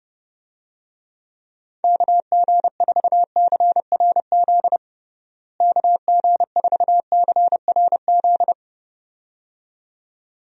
Morse code, heard as a keyed tone, about 30 wpm, 700 Hz. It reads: KG4CRZ KG4CRZ
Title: USA Call Signs - 6 Characters - Repeated 2x - 30wpm